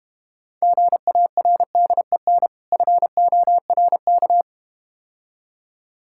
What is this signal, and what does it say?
Morse code, heard as a keyed tone, about 33 wpm, 700 Hz. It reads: GARDEN FORK